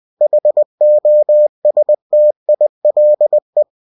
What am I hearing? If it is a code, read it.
Morse code, heard as a keyed tone, about 20 wpm, 600 Hz. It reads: HOSTILE
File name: cw-books / winnie-the-pooh / winnie-20wpm-page-0022